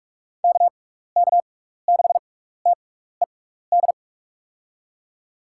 Morse code, heard as a keyed tone, about 41 wpm, 700 Hz. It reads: K K 6 T E D
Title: USA Call Signs - 6 Characters - 6x Character Spacing - 45wpm